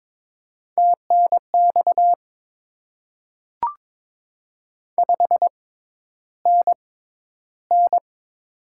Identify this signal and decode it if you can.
Morse code, heard as a keyed tone, about 22 wpm, spaced as for 11 wpm, 700 Hz. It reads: TNX 5NN
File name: Top 20 QSO Elements - 6x Character Spacing - 22wpm